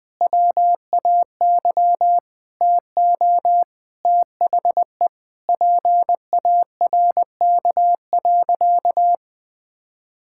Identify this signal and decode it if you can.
Morse code, heard as a keyed tone, about 21 wpm, 700 Hz. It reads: WAY TO THE PARK.
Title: Sentences with 4-Word Phrases - Spoken Beginning - 20wpm